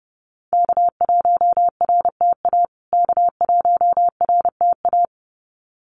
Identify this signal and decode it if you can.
Morse code, heard as a keyed tone, about 30 wpm, 700 Hz. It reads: K1RTA K1RTA
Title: USA Call Signs Repeated 2x 30wpm